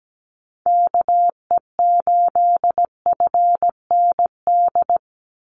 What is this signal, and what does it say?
Morse code, heard as a keyed tone, about 17 wpm, 700 Hz. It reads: KE8FND
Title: USA Call Signs 17wpm